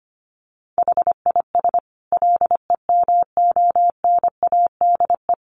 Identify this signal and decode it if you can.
Morse code, heard as a keyed tone, about 25 wpm, 700 Hz. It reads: HIS LEMONADE
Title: Additional Easy Phrases 25wpm